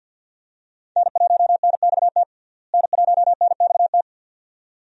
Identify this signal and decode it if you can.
Morse code, heard as a keyed tone, about 52 wpm, 700 Hz. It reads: N1NXT N1NXT
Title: USA Call Signs - 5 Characters - Repeated 2x - 50wpm